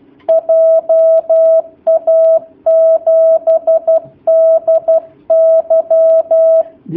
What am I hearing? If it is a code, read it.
Morse code, heard as a keyed tone, about 12 wpm, 650 Hz. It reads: JA7DY
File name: jh8cbh / 2019cw / sp11- w